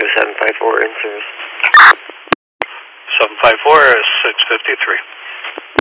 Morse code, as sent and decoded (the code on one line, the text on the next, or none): none